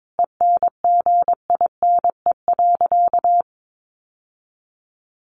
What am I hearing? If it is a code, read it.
Morse code, heard as a keyed tone, about 22 wpm, 700 Hz. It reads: ENGINE.
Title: Sentences from Top 1500 Words 22wpm